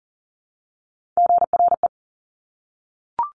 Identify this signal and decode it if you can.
Morse code, heard as a keyed tone, about 41 wpm, 700 Hz. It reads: GRE
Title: Top Three Letter Combinations in Words 40wpm